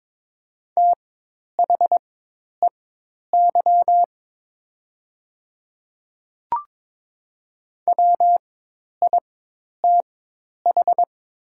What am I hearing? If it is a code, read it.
Morse code, heard as a keyed tone, about 22 wpm, spaced as for 10 wpm, 700 Hz. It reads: THEY WITH
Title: 4x Character Spacing - Top 100 Words - 22wpm